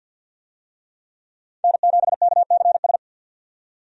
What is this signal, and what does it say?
Morse code, heard as a keyed tone, about 50 wpm, 700 Hz. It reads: N7KCS
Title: USA Call Signs - 5 Characters - 50wpm